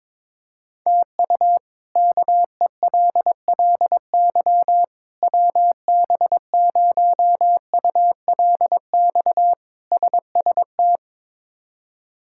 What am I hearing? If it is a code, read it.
Morse code, heard as a keyed tone, about 22 wpm, 700 Hz. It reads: TU KELLY WB0ULX SST
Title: K1USN SST QSO Practice - 22wpm